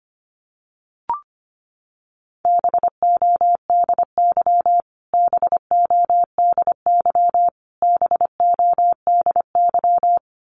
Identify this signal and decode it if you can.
Morse code, heard as a keyed tone, about 25 wpm, 700 Hz. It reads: BODY BODY BODY
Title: Top 500 Words Repeated 25wpm